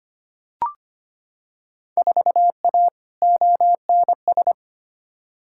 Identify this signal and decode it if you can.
Morse code, heard as a keyed tone, about 25 wpm, 700 Hz. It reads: 4A ONS
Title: Field Day Exchanges - 25wpm